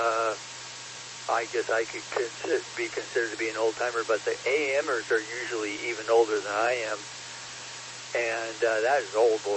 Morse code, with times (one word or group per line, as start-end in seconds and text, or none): none